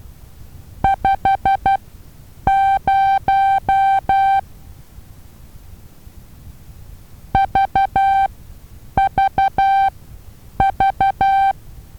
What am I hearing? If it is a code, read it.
Morse code, heard as a keyed tone, about 12 wpm, spaced as for 8 wpm, 780 Hz. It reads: 50 VVV
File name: Taviro HA6VA B